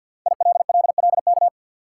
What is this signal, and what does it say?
Morse code, heard as a keyed tone, about 50 wpm, 700 Hz. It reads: ILLLK
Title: Code Groups - 5 Letters - 50wpm